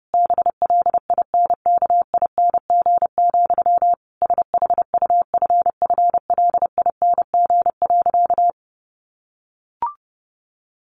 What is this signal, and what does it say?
Morse code, heard as a keyed tone, about 28 wpm, 700 Hz. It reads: BLINKING, SHUFFLING.
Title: A Sound Of Thunder 30wpm